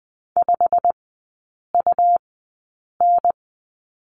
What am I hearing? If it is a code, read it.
Morse code, heard as a keyed tone, about 20 wpm, spaced as for 8 wpm, 690 Hz. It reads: 5UN
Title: 159 Lesson 26e - Letter g - Three Characters 20wpm